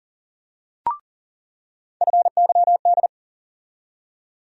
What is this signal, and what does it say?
Morse code, heard as a keyed tone, about 40 wpm, 700 Hz. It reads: FYD
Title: USA Call Sign Postfixes - 3 Letters 40wpm